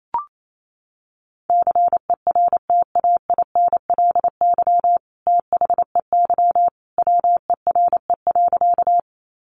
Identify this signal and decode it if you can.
Morse code, heard as a keyed tone, about 28 wpm, 700 Hz. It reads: CERTAINLY THEY WERE.
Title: Sentences from Top 900 Words 28wpm